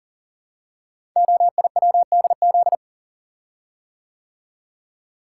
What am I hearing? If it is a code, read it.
Morse code, heard as a keyed tone, about 41 wpm, 700 Hz. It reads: OIWDZ
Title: Code Groups - 5 Letters - 40wpm